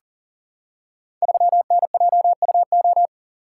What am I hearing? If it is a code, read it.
Morse code, heard as a keyed tone, about 40 wpm, 700 Hz. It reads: 3NJUO
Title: Code Groups - 5 Letter:Numbers - 40wpm